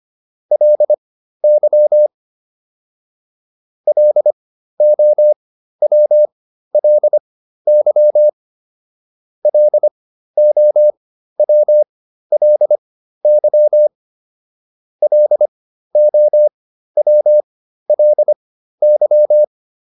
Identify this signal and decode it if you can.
Morse code, heard as a keyed tone, about 25 wpm, spaced as for 13 wpm, 600 Hz. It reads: LY LOWLY LOWLY LOWLY
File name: LY5 13